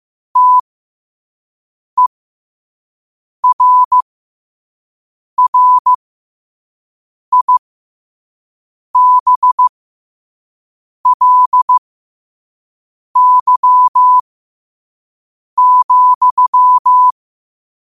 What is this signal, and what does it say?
Morse code, heard as a keyed tone, about 15 wpm, spaced as for 5 wpm, 1000 Hz. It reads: TERRIBLY,
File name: 5wpm